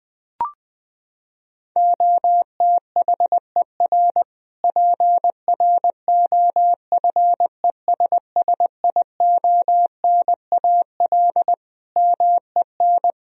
Morse code, heard as a keyed tone, about 20 wpm, 700 Hz. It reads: OTHER PROFESSIONAL MEN
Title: Sentences from Top 1000 Words - Review Part 4 - 20wpm